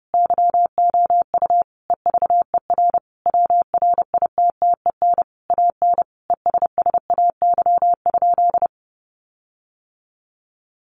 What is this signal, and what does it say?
Morse code, heard as a keyed tone, about 30 wpm, 700 Hz. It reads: YOU EVER WRITTEN AN ESSAY?
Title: Sentences from Top 1500 Words 30wpm